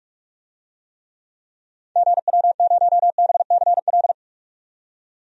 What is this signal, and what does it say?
Morse code, heard as a keyed tone, about 42 wpm, 700 Hz. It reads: GW0BCL
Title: DXCC-100 Call Signs - 6 Characters - 45wpm